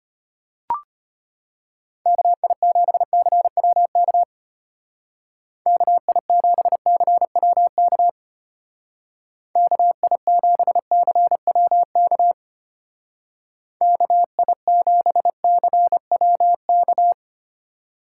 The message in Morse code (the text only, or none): KI7CWK KI7CWK KI7CWK KI7CWK